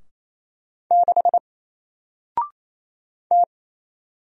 Morse code, heard as a keyed tone, about 27 wpm, 700 Hz. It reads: 6 T
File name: Single-Letter-Number-28wpm